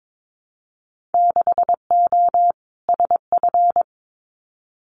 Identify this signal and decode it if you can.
Morse code, heard as a keyed tone, about 22 wpm, 700 Hz. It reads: 6O SF
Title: Winter Field Day Exchanges Encore 22wpm